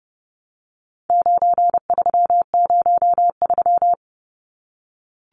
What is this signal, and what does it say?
Morse code, heard as a keyed tone, about 30 wpm, 700 Hz. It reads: 9303
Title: Numbers - 4-Digit 30wpm